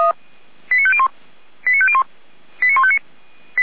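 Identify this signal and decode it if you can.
Five-tone selective call (ZVEI), heard as quick runs of five tones, tones of about 70 ms. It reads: 89501 89501 89159
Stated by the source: ZVEI five-tone call received by SDR.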